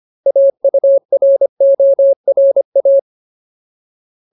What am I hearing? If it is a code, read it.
Morse code, heard as a keyed tone, about 25 wpm, 550 Hz. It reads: AURORA
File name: Ultimate Word List / 25WPM Ultimate